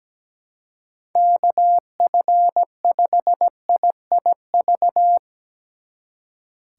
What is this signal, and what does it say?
Morse code, heard as a keyed tone, about 19 wpm, 700 Hz. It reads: KF5IIV
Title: Speed Racing - USA 2x3 Call Signs 17wpm